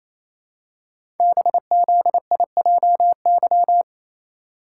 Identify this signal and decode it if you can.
Morse code, heard as a keyed tone, about 28 wpm, 700 Hz. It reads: BZIJY
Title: Code Groups - 5 Letters - 28wpm